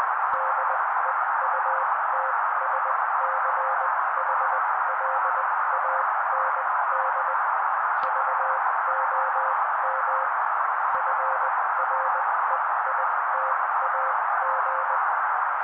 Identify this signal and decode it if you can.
Morse code, heard as a keyed tone, about 20 wpm, spaced as for 16 wpm, 530 Hz. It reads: DEUTSCHLAND VOM FREITAG
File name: daily / 16wpm noise